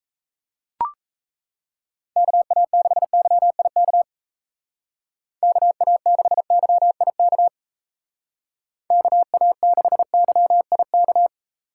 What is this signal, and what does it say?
Morse code, heard as a keyed tone, about 34 wpm, 700 Hz. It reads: KA6YIK KA6YIK KA6YIK
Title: Speed Racing - USA 2x3 Call Signs 28wpm